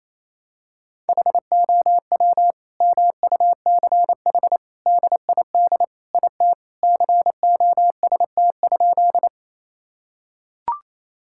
Morse code, heard as a keyed tone, about 28 wpm, 700 Hz. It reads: HOW MUCH DID IT COST?
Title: Sentences from Top 1000 Words - Review Part 5 - 28wpm